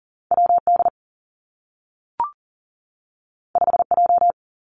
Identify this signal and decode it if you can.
Morse code, heard as a keyed tone, about 40 wpm, 700 Hz. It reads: WD 5J
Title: Two Challenging Letter-Number Combinations 40wpm